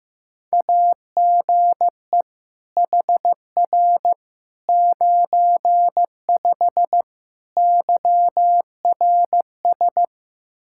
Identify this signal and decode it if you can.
Morse code, heard as a keyed tone, about 16 wpm, 700 Hz. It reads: AGE HR 95 YRS
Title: Practice QSO - Exchange 3 - More Info - 15wpm